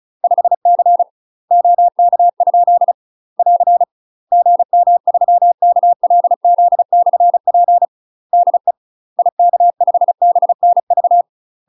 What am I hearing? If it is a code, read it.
Morse code, heard as a keyed tone, about 34 wpm, 700 Hz. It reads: 5C OK? <AR> GM3KLZ/P DE IK5BNV
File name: UK MASTER QSOs TRAINING / 35WPM / UK MASTER QSOs 0026